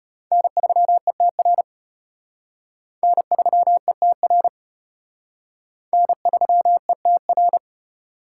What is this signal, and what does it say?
Morse code, heard as a keyed tone, about 30 wpm, 700 Hz. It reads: N3ETR N3ETR N3ETR
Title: Speed Racing - USA 1x3 Call Signs 25wpm